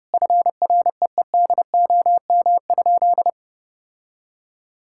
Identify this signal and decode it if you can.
Morse code, heard as a keyed tone, about 30 wpm, 700 Hz. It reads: FREEDOM?
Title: Sentences from Top 1600 Words 30wpm